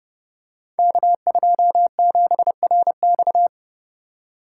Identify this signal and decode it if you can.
Morse code, heard as a keyed tone, about 30 wpm, 700 Hz. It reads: K27RX
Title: Code Groups - 5 Letter:Numbers - 30wpm